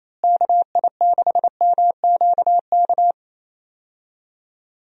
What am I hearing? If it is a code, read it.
Morse code, heard as a keyed tone, about 28 wpm, 700 Hz. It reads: KI6MQK
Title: USA Call Signs - 6 Characters - Repeated 3x - 28wpm